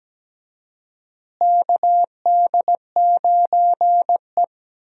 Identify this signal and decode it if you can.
Morse code, heard as a keyed tone, about 17 wpm, 700 Hz. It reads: KD9E
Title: USA Call Signs - 4 Characters - Repeated 3x - 17wpm